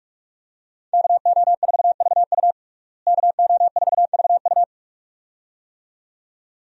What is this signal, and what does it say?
Morse code, heard as a keyed tone, about 41 wpm, 700 Hz. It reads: KO4VU KO4VU
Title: USA Call Signs - 5 Characters - Repeated 2x - 45wpm